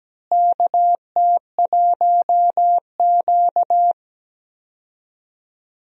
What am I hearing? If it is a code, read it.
Morse code, heard as a keyed tone, about 17 wpm, 700 Hz. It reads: KT1Q
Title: USA Call Signs - 4 Characters - Repeated 3x - 17wpm